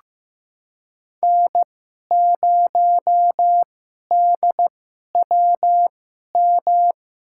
Morse code, heard as a keyed tone, about 15 wpm, spaced as for 11 wpm, 700 Hz. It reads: N0DWM
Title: USA Call Signs - 5 Characters - 2x Character Spacing - 15wpm